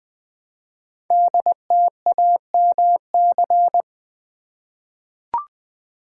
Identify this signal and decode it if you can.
Morse code, heard as a keyed tone, about 20 wpm, 700 Hz. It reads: DTAMC
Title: Code Groups - 5 Letters - 20wpm